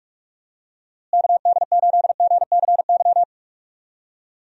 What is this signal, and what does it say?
Morse code, heard as a keyed tone, about 44 wpm, 700 Hz. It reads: KD8GCY
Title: USA Call Signs - 6 Characters - Repeated 3x - 45wpm